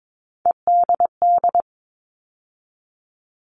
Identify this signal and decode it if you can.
Morse code, heard as a keyed tone, about 22 wpm, 700 Hz. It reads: EDD
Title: All Cut Numbers - 3-Digit 22wpm